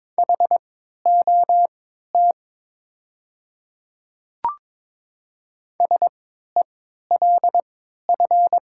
Morse code, heard as a keyed tone, about 22 wpm, spaced as for 13 wpm, 700 Hz. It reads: HOT SELF
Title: 3x Character Spacing - Top 200 Words - 22wpm